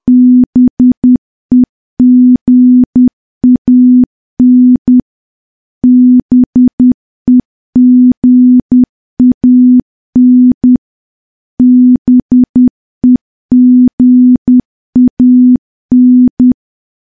Morse code, heard as a keyed tone, about 10 wpm, 250 Hz. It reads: BEGAN BEGAN BEGAN
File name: morse / morse-training-10WPM